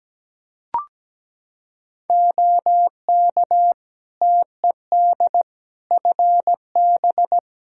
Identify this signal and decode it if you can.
Morse code, heard as a keyed tone, about 17 wpm, 700 Hz. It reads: OK TED FB
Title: Practice QSO - Exchange 2 - Info - 17wpm